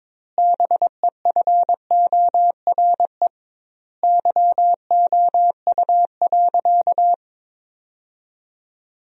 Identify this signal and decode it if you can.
Morse code, heard as a keyed tone, about 22 wpm, 700 Hz. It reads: BEFORE YOU.
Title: 2x Word-Spacing - Sentences from Top 200 Words - 22wpm